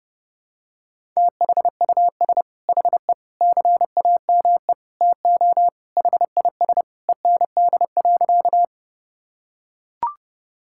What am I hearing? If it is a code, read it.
Morse code, heard as a keyed tone, about 30 wpm, 700 Hz. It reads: THUS HE CAME TO HIS END.
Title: Sentences from Top 1000 Words - Review Part 5 - 30wpm